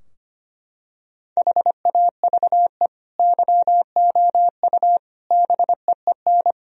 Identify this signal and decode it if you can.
Morse code, heard as a keyed tone, about 25 wpm, 700 Hz. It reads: HAVE YOU BEEN
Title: Sentences from Top 1000 Words - Review Part 4 - 25wpm